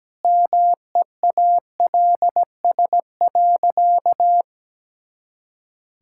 Morse code, heard as a keyed tone, about 17 wpm, 700 Hz. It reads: MEALS.
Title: Pangram Challenge v2 - 17wpm